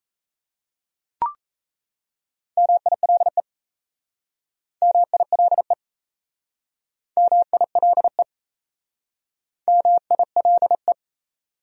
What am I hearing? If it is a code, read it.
Morse code, heard as a keyed tone, about 37 wpm, 700 Hz. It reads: MILE MILE MILE MILE